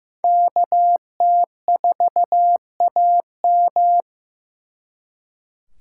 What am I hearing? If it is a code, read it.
Morse code, heard as a keyed tone, about 18 wpm, 700 Hz. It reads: KT4AM
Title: Speed Racing - USA 2x2 Call Signs 15wpm